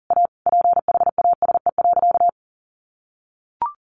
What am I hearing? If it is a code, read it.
Morse code, heard as a keyed tone, about 40 wpm, 700 Hz. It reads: A PHASE.
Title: Sentences from Top 1200 Words 40wpm